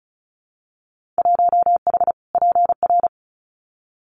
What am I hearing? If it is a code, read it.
Morse code, heard as a keyed tone, about 36 wpm, 700 Hz. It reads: 1H PR
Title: Winter Field Day Exchanges Encore 35wpm